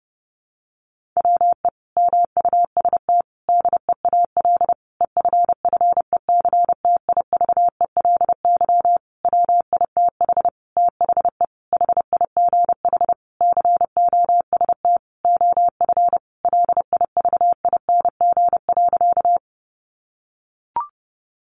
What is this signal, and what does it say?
Morse code, heard as a keyed tone, about 28 wpm, 700 Hz. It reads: WE MUST DEAL EFFECTIVELY WITH THE HIGH COST OF LIVING.
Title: Sentences from Top 1400 Words 30wpm